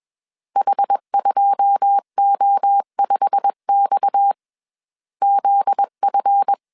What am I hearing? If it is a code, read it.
Morse code, heard as a keyed tone, about 21 wpm, 800 Hz. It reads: H2O5X ZF